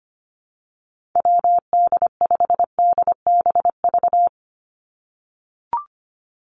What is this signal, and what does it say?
Morse code, heard as a keyed tone, about 25 wpm, 700 Hz. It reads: WD5DBV
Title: Call Signs - 25wpm